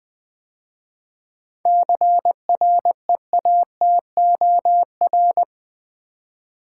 Morse code, heard as a keyed tone, about 20 wpm, 700 Hz. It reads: CREATOR
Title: 7 Letter Words - First 3 Letters Given - 20wpm